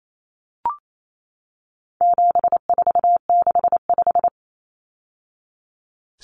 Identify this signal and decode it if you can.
Morse code, heard as a keyed tone, about 28 wpm, 700 Hz. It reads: 7465
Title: Numbers - 4-Digit 28wpm